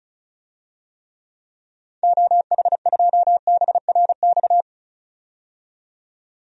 Morse code, heard as a keyed tone, about 36 wpm, 700 Hz. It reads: OH2BRX
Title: DXCC-100 Call Signs - 6 Characters - 35wpm